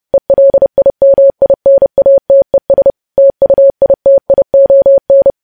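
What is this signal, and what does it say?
Morse code, heard as a keyed tone, about 30 wpm, 550 Hz. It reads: ELIMINATES TUITION
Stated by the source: News headlines sent in Morse code at 30.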